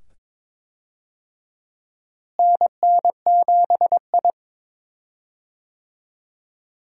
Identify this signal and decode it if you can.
Morse code, heard as a keyed tone, about 23 wpm, 700 Hz. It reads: NN7I
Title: USA Call Signs - 4 Characters - Repeated 2x - 22wpm